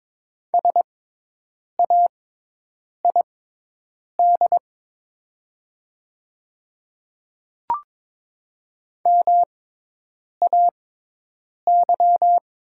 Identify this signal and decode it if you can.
Morse code, heard as a keyed tone, about 22 wpm, spaced as for 8 wpm, 700 Hz. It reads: SAID MAY
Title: 6x Character Spacing - Top 100 Words - 22wpm